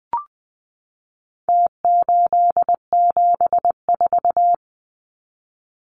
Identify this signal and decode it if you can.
Morse code, heard as a keyed tone, about 20 wpm, 700 Hz. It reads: T874
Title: Numbers - Contest Style - 4-Digit 20wpm